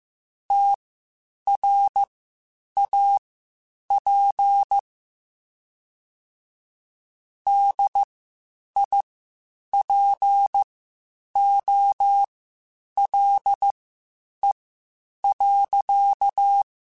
Morse code, heard as a keyed tone, about 15 wpm, spaced as for 8 wpm, 800 Hz. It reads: TRAP DIPOLE.
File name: (spaced) / QSOs Spaced-mp3s / WD0FBT 1 @13 sp